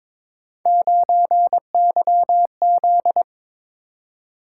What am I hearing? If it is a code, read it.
Morse code, heard as a keyed tone, about 22 wpm, 700 Hz. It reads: 9YZ
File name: practice / 22wpm / Three Letter-Number Combinations - 22wpm